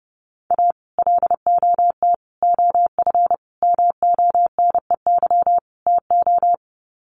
Morse code, heard as a keyed tone, about 30 wpm, 700 Hz. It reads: A LOT OF MONEY TO
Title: Sets of 5 Words Encore2 30wpm